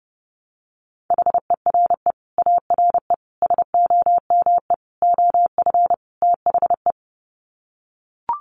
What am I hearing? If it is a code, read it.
Morse code, heard as a keyed tone, about 30 wpm, 700 Hz. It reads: HERE ARE SOME OF THE
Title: Sets of 5 Words Encore 30wpm